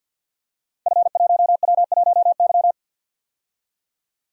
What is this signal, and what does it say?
Morse code, heard as a keyed tone, about 48 wpm, 700 Hz. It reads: F1W1Y
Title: Code Groups - 5 Letter:Numbers - 50wpm